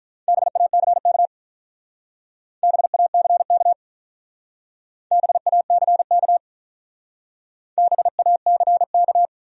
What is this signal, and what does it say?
Morse code, heard as a keyed tone, about 43 wpm, 700 Hz. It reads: BACK BACK BACK BACK
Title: Speed-Racing Top 300 Words 35wpm